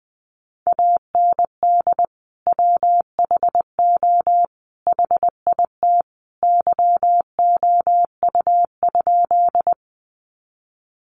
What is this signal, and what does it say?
Morse code, heard as a keyed tone, about 20 wpm, 700 Hz. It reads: AND WHO HIT YOU?
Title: Sentences from Top 1200 Words 20wpm